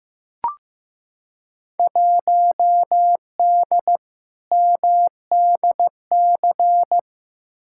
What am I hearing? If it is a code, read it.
Morse code, heard as a keyed tone, about 15 wpm, 700 Hz. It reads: 1D MDC
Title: Field Day Exchanges - 15wpm